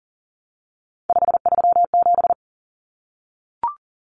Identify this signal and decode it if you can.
Morse code, heard as a keyed tone, about 40 wpm, 700 Hz. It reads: 537
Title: Numbers - 3-Digit 40wpm